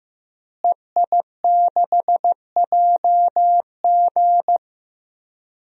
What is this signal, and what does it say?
Morse code, heard as a keyed tone, about 16 wpm, 700 Hz. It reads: EI6JG
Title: DXCC-100 Call Signs - 5 Characters - Repeated 3x - 15wpm